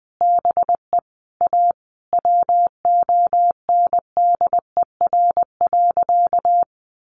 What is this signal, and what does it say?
Morse code, heard as a keyed tone, about 20 wpm, 700 Hz. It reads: BE A WONDER.
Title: Sentences from Top 500 Words 20wpm